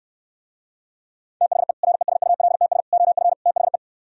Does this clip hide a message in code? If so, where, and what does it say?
Yes: Morse code from 1.4 to 3.8 s, heard as a keyed tone, about 67 wpm, 700 Hz. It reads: THE RESULTS OF THE